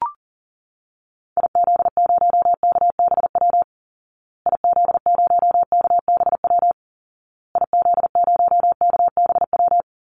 Morse code, heard as a keyed tone, about 40 wpm, 700 Hz. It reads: IZ0KBW IZ0KBW IZ0KBW
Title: Call Signs Repeated 2x - 40wpm